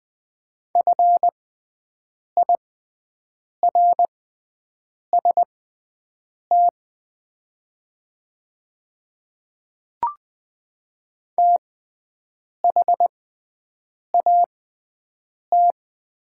Morse code, heard as a keyed tone, about 20 wpm, spaced as for 7 wpm, 700 Hz. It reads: FIRST THAT
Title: 6x Character Spacing - Top 200 Words - 20wpm